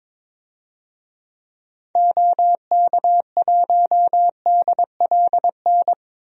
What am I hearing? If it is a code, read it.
Morse code, heard as a keyed tone, about 22 wpm, 700 Hz. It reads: OK1DLN
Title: DXCC-100 Call Signs - 6 Characters - Repeated 3x - 22wpm